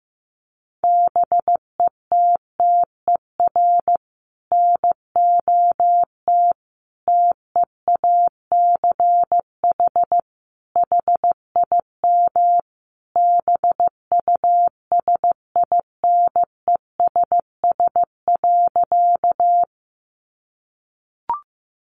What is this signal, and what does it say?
Morse code, heard as a keyed tone, about 15 wpm, 700 Hz. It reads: BETTER NOT TEACH HIM BUSINESS.
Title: Sentences from Top 600 Words 15wpm